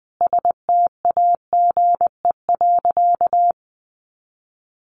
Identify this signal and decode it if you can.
Morse code, heard as a keyed tone, about 20 wpm, 700 Hz. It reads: STAGE.
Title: Sentences from Top 1200 Words 20wpm